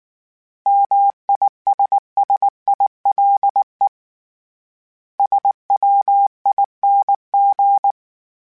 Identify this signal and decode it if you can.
Morse code, heard as a keyed tone, about 19 wpm, 800 Hz. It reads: MISSILE SWING
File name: Short Runs / Short Runs-mp3s / Words @18